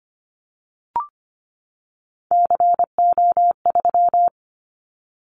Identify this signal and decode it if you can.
Morse code, heard as a keyed tone, about 25 wpm, 700 Hz. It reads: CO3
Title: Callsign Prefixes for Top 100 DXCC - 25wpm